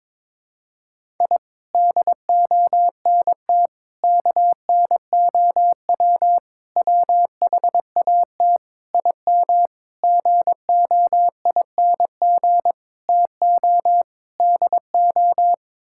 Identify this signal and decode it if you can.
Morse code, heard as a keyed tone, about 22 wpm, 700 Hz. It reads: I DONT KNOW WHAT IM GOING TO DO
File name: practice / 22wpm / Sets of 8 Words - 22wpm